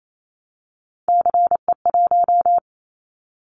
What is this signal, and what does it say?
Morse code, heard as a keyed tone, about 28 wpm, 700 Hz. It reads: CE1
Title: Callsign Prefixes for Top 100 DXCC - 28wpm